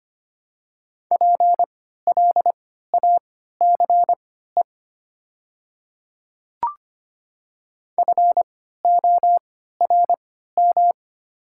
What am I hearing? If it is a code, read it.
Morse code, heard as a keyed tone, about 25 wpm, spaced as for 14 wpm, 690 Hz. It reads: PLACE FORM
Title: 3x Character Spacing - Top 200 Words - 25wpm